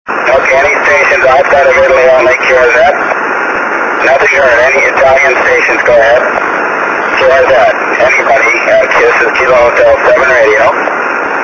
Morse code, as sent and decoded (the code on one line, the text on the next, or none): none